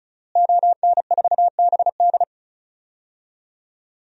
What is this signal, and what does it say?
Morse code, heard as a keyed tone, about 36 wpm, 700 Hz. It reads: ON4BD
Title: DXCC-100 Call Signs - 5 Characters - Repeated 2x - 35wpm